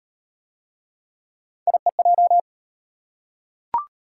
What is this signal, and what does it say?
Morse code, heard as a keyed tone, about 38 wpm, 700 Hz. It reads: IEJ